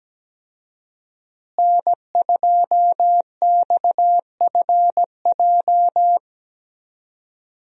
Morse code, heard as a keyed tone, about 17 wpm, 700 Hz. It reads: N2XFJ
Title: USA Call Signs - 5 Characters - Repeated 3x - 17wpm